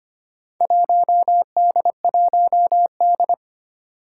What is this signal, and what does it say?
Morse code, heard as a keyed tone, about 25 wpm, 700 Hz. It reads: 1D1D
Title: Four Challenging Letter-Number Combinations 25wpm